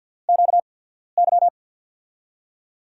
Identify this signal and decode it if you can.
Morse code, heard as a keyed tone, about 48 wpm, spaced as for 14 wpm, 700 Hz. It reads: QY
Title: USA Call Signs - 5 Characters - 8x Character Spacing - 50wpm